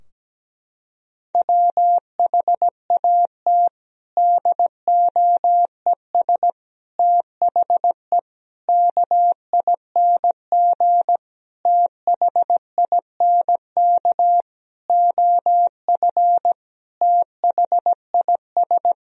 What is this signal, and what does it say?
Morse code, heard as a keyed tone, about 17 wpm, 700 Hz. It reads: WHAT DOES THE KING THINK OF THIS
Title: Sentences from Top 1000 Words - Review Part 2 - 17wpm